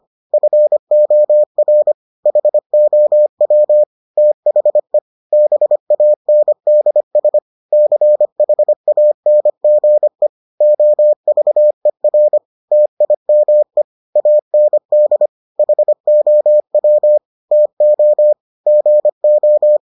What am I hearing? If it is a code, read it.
Morse code, heard as a keyed tone, about 25 wpm, 600 Hz. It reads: FOR HOW THE BANDS CHANGE OVER TIME AND HOW TO GO